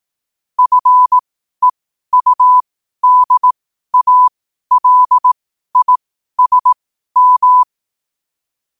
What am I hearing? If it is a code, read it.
Morse code, heard as a keyed tone, about 18 wpm, spaced as for 12 wpm, 1000 Hz. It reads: FEUDALISM
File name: qotdfiles / 12wpm